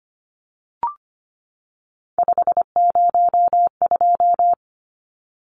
Morse code, heard as a keyed tone, about 25 wpm, 700 Hz. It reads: 502
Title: Numbers - 3-Digit 25wpm